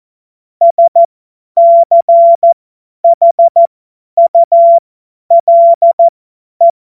Morse code, heard as a keyed tone, about 14 wpm, spaced as for 10 wpm, 680 Hz. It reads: SCHULE
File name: Morsekurs / Witze / Witz 14. März 2023 50 F70